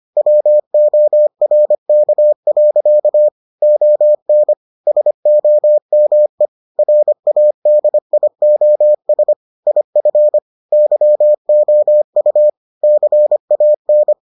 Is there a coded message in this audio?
Yes: Morse code from 0.2 to 14.1 s, heard as a keyed tone, about 25 wpm, 600 Hz. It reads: WORK. ON SOME RADIOS IF YOU CAN